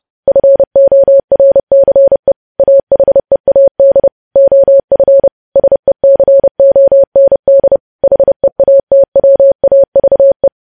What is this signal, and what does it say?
Morse code, heard as a keyed tone, about 30 wpm, 550 Hz. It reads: FORCE AHEAD OF SECOND HEATWAVE
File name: podcast / News headlines-1x-30x30